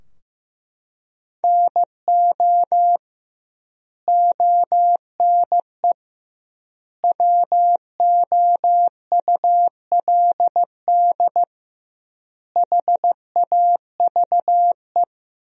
Morse code, heard as a keyed tone, about 15 wpm, 700 Hz. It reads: NO ONE WOULD HAVE
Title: 2x Word-Spacing - Sentences from Top 300 Words - 15wpm